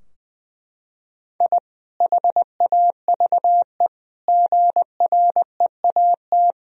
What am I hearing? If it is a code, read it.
Morse code, heard as a keyed tone, about 20 wpm, 700 Hz. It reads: I HAVE GREAT
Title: Sentences from Top 1000 Words - Review Part 2 - 20wpm